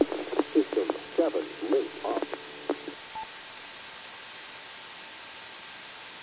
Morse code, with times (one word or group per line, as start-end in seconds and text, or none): none